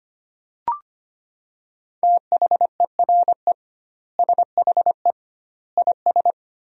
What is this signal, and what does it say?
Morse code, heard as a keyed tone, about 25 wpm, 700 Hz. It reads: THERE SHE IS
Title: 2x Word-Spacing - Sentences from Top 200 Words - 25wpm